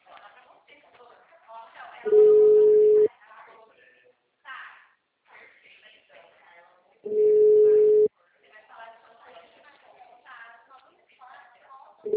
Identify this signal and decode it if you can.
Ringback tone (Europe, 425 Hz), with speech.